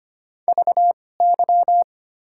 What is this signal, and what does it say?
Morse code, heard as a keyed tone, about 25 wpm, spaced as for 18 wpm, 700 Hz. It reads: VY
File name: Top 90 QSO Elements - 2x Character Spacing - 25wpm